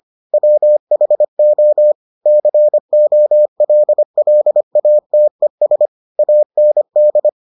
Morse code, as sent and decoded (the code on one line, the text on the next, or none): .-- .... --- / -.-. --- .-.. .-.. .- - . ... / .- -. -..
WHO COLLATES AND